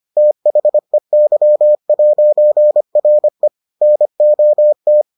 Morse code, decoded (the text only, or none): THEY'RE NOT